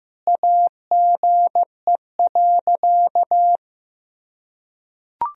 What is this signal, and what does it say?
Morse code, heard as a keyed tone, about 15 wpm, 700 Hz. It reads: AGE.
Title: Sentences from Top 1000 Words - Review Part 3 - 15wpm